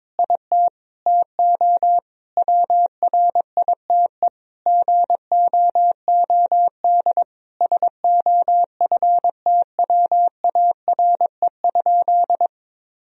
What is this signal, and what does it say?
Morse code, heard as a keyed tone, about 22 wpm, 700 Hz. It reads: IT TO WRITE GOOD SOFTWARE?
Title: Sentences from Top 1000 Words - Review Part 3 - 22wpm